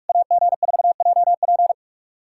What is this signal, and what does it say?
Morse code, heard as a keyed tone, about 44 wpm, 700 Hz. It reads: AG4JP
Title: SOTA QSO Practice 45wpm